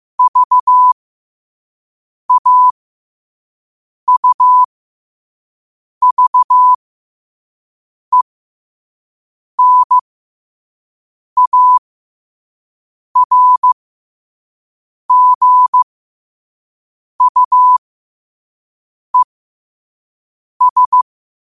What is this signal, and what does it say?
Morse code, heard as a keyed tone, about 15 wpm, spaced as for 5 wpm, 1000 Hz. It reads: VAUVENARGUES
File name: qotdfiles / 5wpm